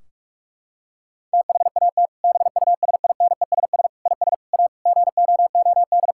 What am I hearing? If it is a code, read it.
Morse code, heard as a keyed tone, about 41 wpm, 700 Hz. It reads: THAT BUSINESS IS A GOOD
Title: Sentences from Top 1600 Words 45wpm